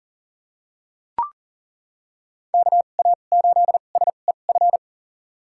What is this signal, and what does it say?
Morse code, heard as a keyed tone, about 40 wpm, spaced as for 29 wpm, 700 Hz. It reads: KA8SEF
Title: USA Call Signs - 2x Character Spacing - 40wpm